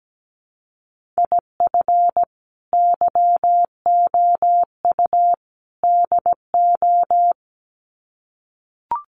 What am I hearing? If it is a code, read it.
Morse code, heard as a keyed tone, about 17 wpm, 700 Hz. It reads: IF YOU DO
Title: Sets of 3 Words 17wpm